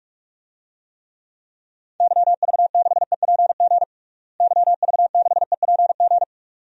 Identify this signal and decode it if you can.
Morse code, heard as a keyed tone, about 46 wpm, 700 Hz. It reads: YV6EPG YV6EPG
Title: DXCC-100 Call Signs - 6 Characters - Repeated 2x - 45wpm